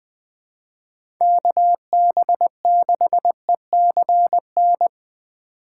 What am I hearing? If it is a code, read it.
Morse code, heard as a keyed tone, about 20 wpm, 700 Hz. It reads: KB6ECN
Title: USA Call Signs - 6 Characters - 3x Character Spacing - 20wpm